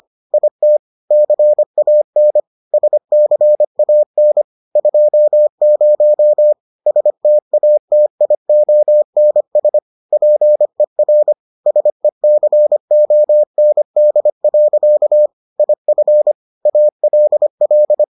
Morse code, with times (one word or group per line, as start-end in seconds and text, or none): 0.3-0.8 IT
1.1-2.4 CAN
2.7-4.4 SCAN
4.8-6.5 20
6.9-9.8 STATIONS
10.1-11.3 PER
11.7-15.3 SECOND.
15.6-16.3 IF
16.7-18.0 ALL